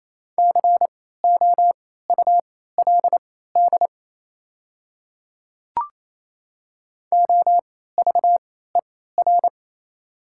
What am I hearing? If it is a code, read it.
Morse code, heard as a keyed tone, about 28 wpm, spaced as for 16 wpm, 700 Hz. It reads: COULD OVER
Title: 3x Character Spacing - Top 100 Words - 28wpm